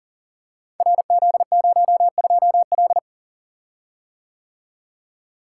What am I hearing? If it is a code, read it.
Morse code, heard as a keyed tone, about 40 wpm, 700 Hz. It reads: RZ02L
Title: Code Groups - 5 Letter:Numbers - 40wpm